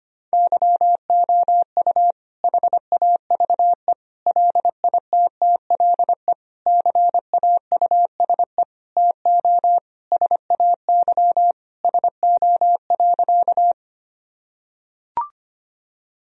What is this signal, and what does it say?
Morse code, heard as a keyed tone, about 25 wpm, 700 Hz. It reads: YOU HAVE LITTLE CAUSE TO SAY SO.
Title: Sentences from Top 1000 Words - Review Part 1 - 25wpm